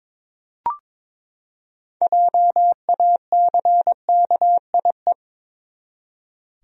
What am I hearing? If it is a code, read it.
Morse code, heard as a keyed tone, about 22 wpm, 700 Hz. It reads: JACKIE
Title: USA Common Names - 22wpm